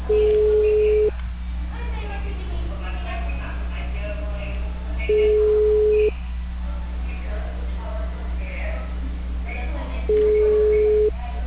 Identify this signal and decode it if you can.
Ringback tone (Europe, 425 Hz), with speech.